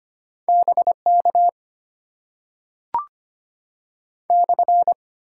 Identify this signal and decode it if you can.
Morse code, heard as a keyed tone, about 25 wpm, 690 Hz. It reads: BK /
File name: Top 90 QSO Elements - 8x Character Spacing - 25wpm